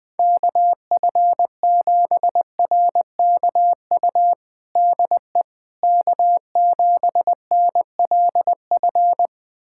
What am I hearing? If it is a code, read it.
Morse code, heard as a keyed tone, about 21 wpm, 700 Hz. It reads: KF7RKU DE K7NLF
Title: USA Callsign DE Callsign 20wpm